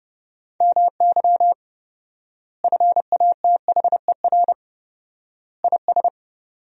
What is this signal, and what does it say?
Morse code, heard as a keyed tone, about 32 wpm, 700 Hz. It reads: MY FATHER IS
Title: 4x Word-Spacing - Sentences from Top 300 Words - 30wpm